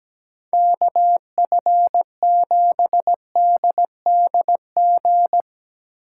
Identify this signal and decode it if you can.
Morse code, heard as a keyed tone, about 19 wpm, 700 Hz. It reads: KF7DDG
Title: Speed Racing - USA 2x3 Call Signs 17wpm